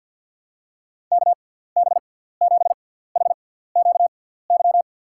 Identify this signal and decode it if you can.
Morse code, heard as a keyed tone, about 48 wpm, spaced as for 17 wpm, 700 Hz. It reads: KB7HQY
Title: USA Call Signs - 6 Characters - 6x Character Spacing - 50wpm